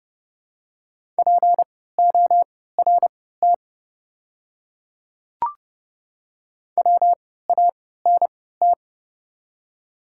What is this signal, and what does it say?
Morse code, heard as a keyed tone, about 30 wpm, spaced as for 17 wpm, 700 Hz. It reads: PORT WANT
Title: 3x Character Spacing - Top 300 Words - 30wpm